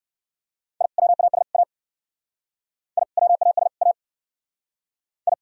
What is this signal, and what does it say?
Morse code, heard as a keyed tone, about 57 wpm, 700 Hz. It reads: I WAS A I WAS A I